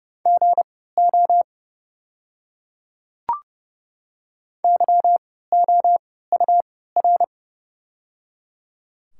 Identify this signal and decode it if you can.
Morse code, heard as a keyed tone, about 29 wpm, spaced as for 17 wpm, 700 Hz. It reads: GO YOUR
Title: 3x Character Spacing - Top 100 Words - 30wpm